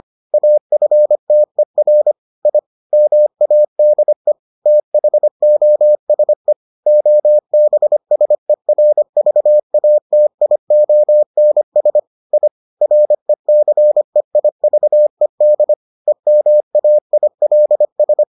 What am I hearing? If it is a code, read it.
Morse code, heard as a keyed tone, about 25 wpm, 600 Hz. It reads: AFTER I MADE THOSE OBSERVATIONS I RECEIVED EMAILS